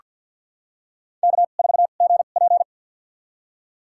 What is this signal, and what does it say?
Morse code, heard as a keyed tone, about 49 wpm, spaced as for 36 wpm, 700 Hz. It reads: K4GP
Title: USA Call Signs - 4 Characters - 2x Character Spacing - 50wpm